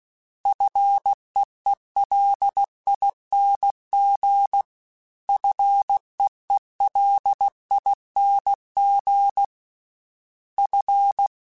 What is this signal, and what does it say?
Morse code, heard as a keyed tone, about 16 wpm, 800 Hz. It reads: FEELING FEELING F
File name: Single F Prefixes 3x2 Twice @15